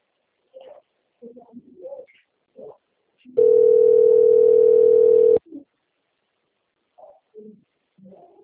Ringback tone (North America), 440 + 480 Hz.